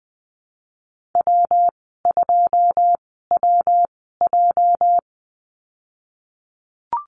Morse code, heard as a keyed tone, about 20 wpm, spaced as for 15 wpm, 700 Hz. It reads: W2WJ
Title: 207 Lesson 33g - Letter j - Call Signs 20wpm